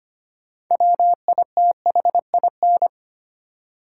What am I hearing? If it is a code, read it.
Morse code, heard as a keyed tone, about 25 wpm, 700 Hz. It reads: WITHIN
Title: Six Letter Words - 25wpm